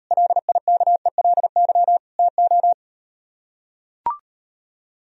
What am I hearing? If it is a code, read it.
Morse code, heard as a keyed tone, about 38 wpm, 700 Hz. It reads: LIKELY TO